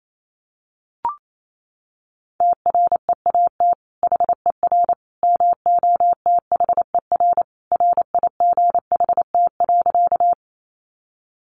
Morse code, heard as a keyed tone, about 28 wpm, 700 Hz. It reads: TREAT HER MOTHER RIGHT.
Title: Sentences from Top 1300 Words 28wpm